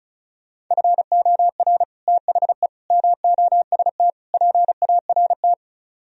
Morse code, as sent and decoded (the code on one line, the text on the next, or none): ..-. --- .-. / - .... / -- --- ... - / .--. .- .-. -
FOR THE MOST PART